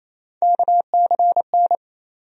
Morse code, heard as a keyed tone, about 28 wpm, 700 Hz. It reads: KCN
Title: USA Call Sign Postfixes - 3 Letters 28wpm